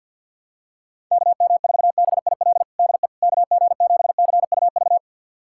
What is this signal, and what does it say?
Morse code, heard as a keyed tone, about 50 wpm, 700 Hz. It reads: KM4BIL DE KG7CUV
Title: USA Callsign DE Callsign - 6 Characters - 50wpm